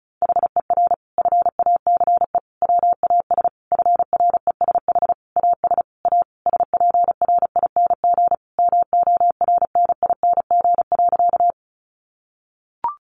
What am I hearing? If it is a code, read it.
Morse code, heard as a keyed tone, about 34 wpm, 700 Hz. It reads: HER FACE WAS FRESH AS A SPRING MORNING.
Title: Sentences from Top 1200 Words 35wpm